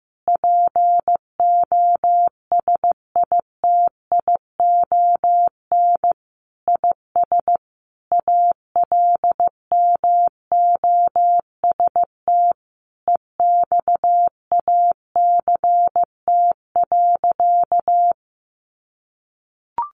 Morse code, heard as a keyed tone, about 15 wpm, 700 Hz. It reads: POSITION IS ALMOST EXACT.
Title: Sentences from Top 1300 Words 15wpm